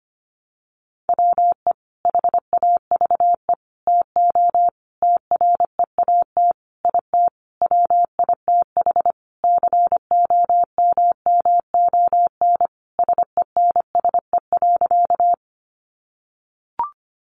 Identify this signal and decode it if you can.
Morse code, heard as a keyed tone, about 25 wpm, 700 Hz. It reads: WE HAVE TO TREAT IT WITH COMMON SENSE.
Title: Sentences from Top 1300 Words 25wpm